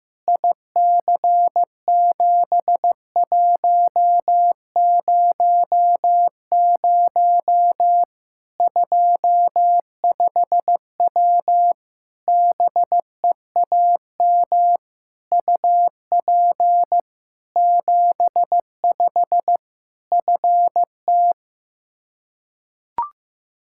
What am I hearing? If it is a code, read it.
Morse code, heard as a keyed tone, about 15 wpm, 700 Hz. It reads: IC7100 25W BEAM UP 75 FT
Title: Practice QSO - Exchange 2 - Info - 15wpm